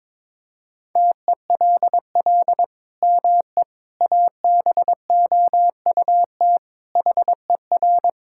Morse code, heard as a keyed tone, about 22 wpm, 700 Hz. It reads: TELL ME ABOUT HER
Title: Sentences from Top 1000 Words - Review Part 5 - 22wpm